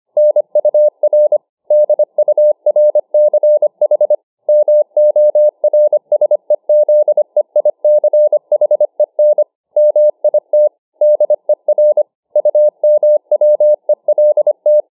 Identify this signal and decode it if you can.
Morse code, heard as a keyed tone, about 25 wpm, 600 Hz. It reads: NUR DURCH MORSEZEICHEN MIT DER UMWELT